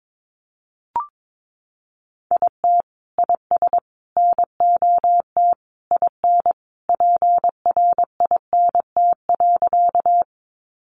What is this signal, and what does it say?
Morse code, heard as a keyed tone, about 22 wpm, 700 Hz. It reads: IT IS NOT IN PRINT.
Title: Sentences from Top 1400 Words 22wpm